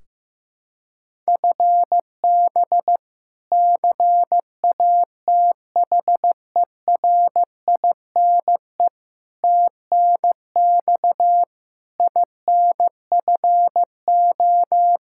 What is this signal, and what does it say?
Morse code, heard as a keyed tone, about 15 wpm, 700 Hz. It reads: FB CATHERINE TNX INFO
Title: K1USN SST QSO Practice 15wpm